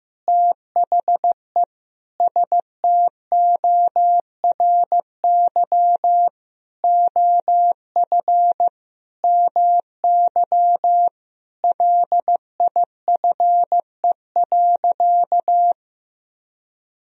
Morse code, heard as a keyed tone, about 16 wpm, 700 Hz. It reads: THE STORY OF MY LIFE.
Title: Sentences from Top 1000 Words - Review Part 1 - 15wpm